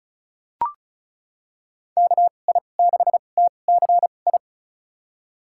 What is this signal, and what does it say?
Morse code, heard as a keyed tone, about 36 wpm, spaced as for 25 wpm, 700 Hz. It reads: KI6TCI
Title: USA Call Signs - 2x Character Spacing - 35wpm